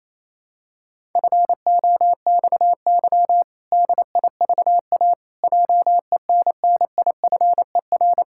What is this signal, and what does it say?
Morse code, heard as a keyed tone, about 28 wpm, 700 Hz. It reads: FOXY DIVA JENNIFER
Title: Pangram Challenge v2 - 28wpm